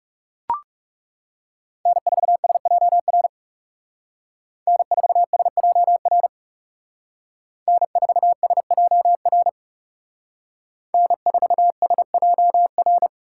Morse code, heard as a keyed tone, about 35 wpm, 700 Hz. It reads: N4SJR N4SJR N4SJR N4SJR
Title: Speed Racing - USA Call Signs 30wpm